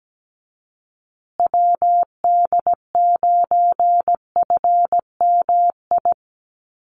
Morse code, heard as a keyed tone, about 17 wpm, 700 Hz. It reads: WD9FMI
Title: USA Call Signs Repeated 3x 17wpm